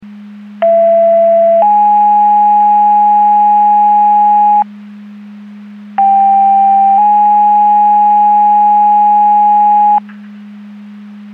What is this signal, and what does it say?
Two-tone page, heard as a one-second tone, then a three-second tone, about 670 then 830 Hz, about 790 then 830 Hz. It is heard from 0.6 to 10.0 s.
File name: Portage County 154.3850 Fire EMS Paging